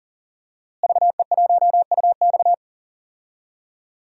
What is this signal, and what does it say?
Morse code, heard as a keyed tone, about 40 wpm, 700 Hz. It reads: VE1UX
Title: DXCC-100 Call Signs - 5 Characters - Repeated 3x - 40wpm